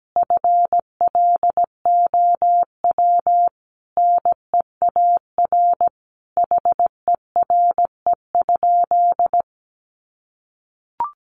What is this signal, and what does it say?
Morse code, heard as a keyed tone, about 17 wpm, 700 Hz. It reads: FLOW NEAR HERE?
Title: Sentences from Top 1200 Words 17wpm